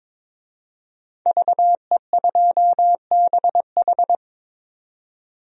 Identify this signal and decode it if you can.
Morse code, heard as a keyed tone, about 22 wpm, 700 Hz. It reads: VE2BH
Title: DXCC-100 Call Signs - 5 Characters - 22wpm